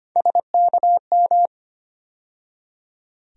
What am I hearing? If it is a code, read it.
Morse code, heard as a keyed tone, about 25 wpm, 700 Hz. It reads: SKM
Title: USA Call Sign Postfixes - 3 Letters 25wpm